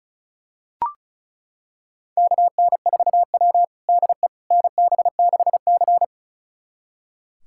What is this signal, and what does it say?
Morse code, heard as a keyed tone, about 36 wpm, 700 Hz. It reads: KN4W DE NB6C
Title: USA Callsign DE Callsign - 4 Characters - 35wpm